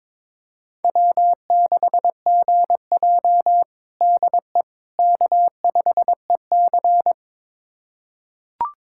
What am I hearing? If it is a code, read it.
Morse code, heard as a keyed tone, about 22 wpm, 700 Hz. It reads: W6GJ DE K5EC
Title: USA Callsign DE Callsign - 4 Characters - 22wpm